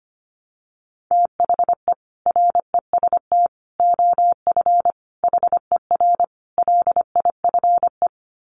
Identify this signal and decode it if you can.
Morse code, heard as a keyed tone, about 25 wpm, 700 Hz. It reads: THE REST OF HER LIFE
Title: Sets of 5 Words Encore 25wpm